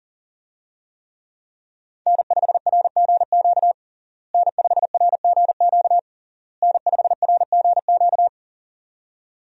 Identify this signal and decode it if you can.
Morse code, heard as a keyed tone, about 40 wpm, 700 Hz. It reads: N5RGQ N5RGQ N5RGQ
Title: USA Call Signs - 5 Characters - Repeated 3x - 40wpm